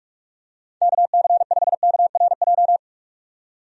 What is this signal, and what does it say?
Morse code, heard as a keyed tone, about 43 wpm, 700 Hz. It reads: KC5KRJ
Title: USA Call Signs - 6 Characters - 45wpm